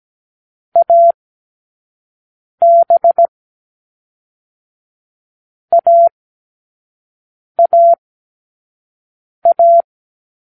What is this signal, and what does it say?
Morse code, heard as a keyed tone, about 17 wpm, 680 Hz. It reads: A B A A A